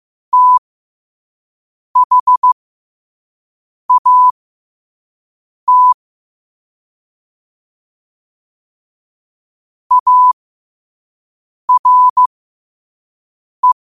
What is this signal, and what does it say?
Morse code, heard as a keyed tone, about 15 wpm, spaced as for 5 wpm, 1000 Hz. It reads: THAT ARE
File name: 5wpm